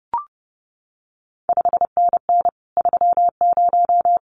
Morse code, heard as a keyed tone, about 30 wpm, 700 Hz. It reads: 5NN 30
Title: CQ WW DX Contest Exchange Encore - 30wpm